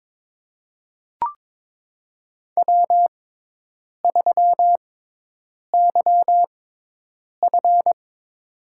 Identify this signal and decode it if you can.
Morse code, heard as a keyed tone, about 21 wpm, spaced as for 8 wpm, 700 Hz. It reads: W3YF
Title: USA Call Signs - 4 Characters - 6x Character Spacing - 22wpm